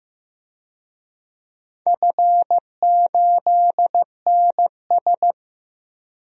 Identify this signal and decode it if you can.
Morse code, heard as a keyed tone, about 15 wpm, 700 Hz. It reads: F8NS
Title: DXCC-100 Call Signs - 4 Characters - Repeated 2x - 15wpm